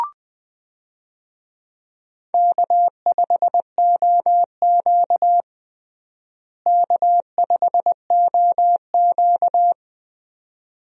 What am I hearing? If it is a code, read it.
Morse code, heard as a keyed tone, about 21 wpm, 700 Hz. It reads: K5OQ K5OQ
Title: USA Call Signs - 4 Characters - Repeated 2x - 20wpm